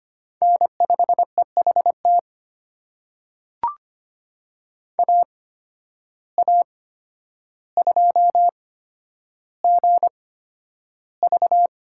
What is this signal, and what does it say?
Morse code, heard as a keyed tone, about 25 wpm, spaced as for 11 wpm, 700 Hz. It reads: N5EHT AA2GV
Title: USA Call Signs - 5 Characters - 8x Character Spacing - 25wpm